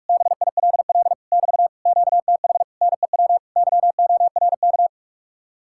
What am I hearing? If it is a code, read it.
Morse code, heard as a keyed tone, about 49 wpm, 700 Hz. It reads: BILL <BT> QTH NEW YORK